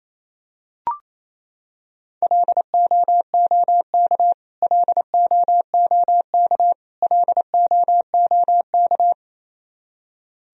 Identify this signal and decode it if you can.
Morse code, heard as a keyed tone, about 28 wpm, 700 Hz. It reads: LOOK LOOK LOOK
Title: Top 100 Words Repeated 28wpm